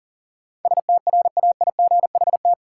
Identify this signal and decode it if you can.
Morse code, heard as a keyed tone, about 40 wpm, 700 Hz. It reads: STRAIGHT
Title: Sentences from Top 1000 Words - New Word Drill - 40wpm